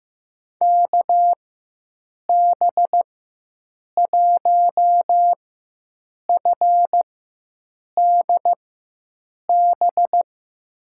Morse code, heard as a keyed tone, about 16 wpm, spaced as for 7 wpm, 700 Hz. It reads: KB1FDB
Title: USA Call Signs - 6 Characters - 4x Character Spacing - 15wpm